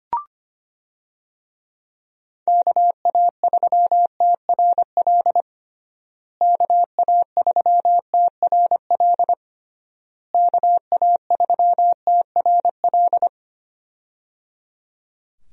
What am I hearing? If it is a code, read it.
Morse code, heard as a keyed tone, about 25 wpm, 700 Hz. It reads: KA3TRL KA3TRL KA3TRL
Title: USA Call Signs - 6 Characters - Repeated 3x - 25wpm